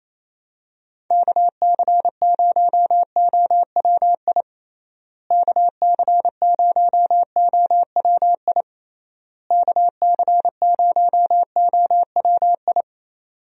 Morse code, heard as a keyed tone, about 28 wpm, 700 Hz. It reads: KC0OWI KC0OWI KC0OWI
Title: USA Call Signs - 6 Characters - Repeated 3x - 28wpm